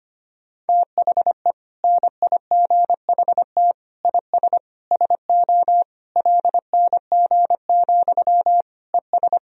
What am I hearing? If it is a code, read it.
Morse code, heard as a keyed tone, about 25 wpm, 700 Hz. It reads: THE NIGHT IS SO LNG, ES
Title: Sentences from Top 500 Words and Abbreviations 25wpm